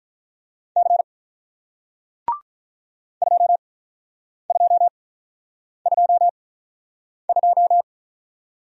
Morse code, heard as a keyed tone, about 48 wpm, spaced as for 9 wpm, 690 Hz. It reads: C 2222